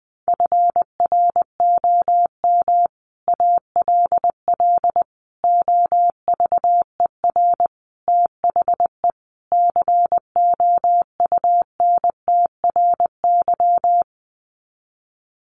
Morse code, heard as a keyed tone, about 20 wpm, 700 Hz. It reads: FROM ALL OVER THE COUNTRY
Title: Sets of 5 Words Encore 20wpm